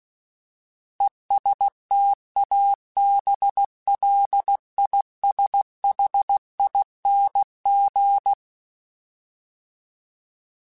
Morse code, heard as a keyed tone, about 16 wpm, 800 Hz. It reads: ESTABLISHING
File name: Alphabetical / Alphabetical mp3s / Es @15